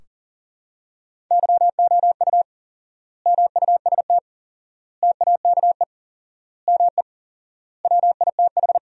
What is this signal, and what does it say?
Morse code, heard as a keyed tone, about 40 wpm, 700 Hz. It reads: YOU MUST TAKE ME WITH
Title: 4x Word-Spacing - Sentences from Top 200 Words - 40wpm